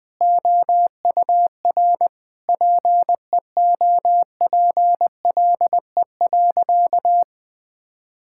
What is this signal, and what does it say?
Morse code, heard as a keyed tone, about 21 wpm, 700 Hz. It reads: OUR PEOPLE.
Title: Sentences from Top 1000 Words - Review Part 5 - 20wpm